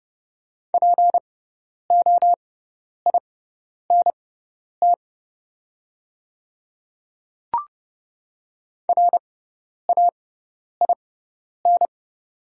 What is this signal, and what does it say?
Morse code, heard as a keyed tone, about 30 wpm, spaced as for 10 wpm, 700 Hz. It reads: POINT RAIN
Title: 6x Character Spacing - Top 300 Words - 30wpm